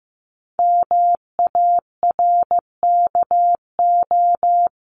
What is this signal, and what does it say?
Morse code, heard as a keyed tone, about 15 wpm, 700 Hz. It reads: MARKO
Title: CWT Exchange 15wpm